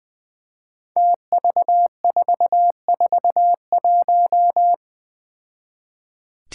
Morse code, heard as a keyed tone, about 20 wpm, 700 Hz. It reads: TV441
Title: Five Challenging Letter-Number Combinations 20wpm